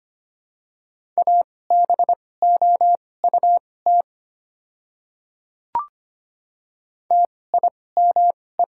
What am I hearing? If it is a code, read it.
Morse code, heard as a keyed tone, about 25 wpm, spaced as for 18 wpm, 700 Hz. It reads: ABOUT TIME
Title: 2x Character Spacing - Top 100 Words - 25wpm